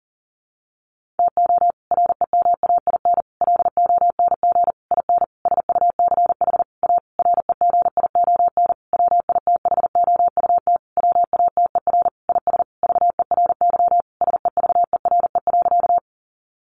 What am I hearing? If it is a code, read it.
Morse code, heard as a keyed tone, about 40 wpm, 700 Hz. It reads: TO REMAIN LONG IN SUCH A REGION WITHOUT WATER IS VERY SEVERE.
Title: Sentences from Top 1500 Words 40wpm